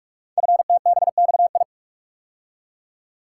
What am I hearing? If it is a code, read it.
Morse code, heard as a keyed tone, about 44 wpm, 700 Hz. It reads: FTBXI
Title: Code Groups - 5 Letters - 45wpm